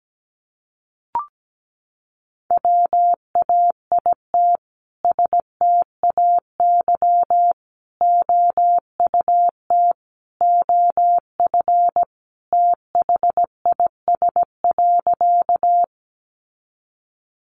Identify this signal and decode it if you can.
Morse code, heard as a keyed tone, about 17 wpm, 700 Hz. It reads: WAIT STAY OUT OF THIS.
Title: A Sound Of Thunder 17wpm